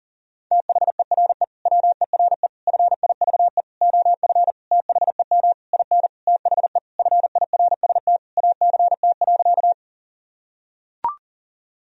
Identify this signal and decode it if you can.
Morse code, heard as a keyed tone, about 40 wpm, 700 Hz. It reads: THERE WERE FIVE OF THEM IN THE FIRST ACT.
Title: Sentences from Top 1000 Words - Review Part 2 - 40wpm